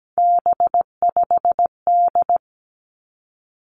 Morse code, heard as a keyed tone, about 17 wpm, 700 Hz. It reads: B5D
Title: Three Challenging Letter-Number Combinations 17wpm